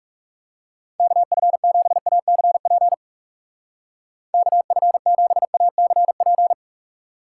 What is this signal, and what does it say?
Morse code, heard as a keyed tone, about 37 wpm, 700 Hz. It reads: KF7ACP KF7ACP
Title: Speed Racing - USA 2x3 Call Signs 30wpm